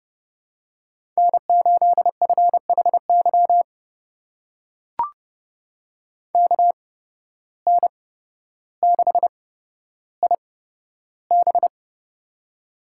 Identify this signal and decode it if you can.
Morse code, heard as a keyed tone, about 32 wpm, spaced as for 13 wpm, 700 Hz. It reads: N8FHY KN6IB
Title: USA Call Signs - 8x Character Spacing - 30wpm